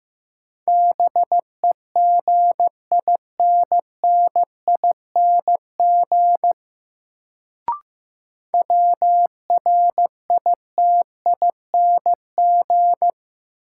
Morse code, heard as a keyed tone, about 15 wpm, 700 Hz. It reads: BEGINNING WRITING
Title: Sentences from Top 800 Words - New Word Drill - 15wpm